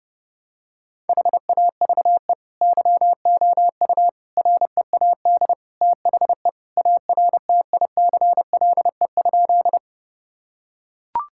Morse code, heard as a keyed tone, about 30 wpm, 700 Hz. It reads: HAVE YOU READ THE ARTICLE?
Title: Sentences from Top 1000 Words - Review Part 5 - 30wpm